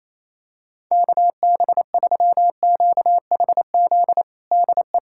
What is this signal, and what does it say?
Morse code, heard as a keyed tone, about 28 wpm, 700 Hz. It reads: KB3QHZ DE
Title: USA Callsign DE Callsign - 6 Characters - 28wpm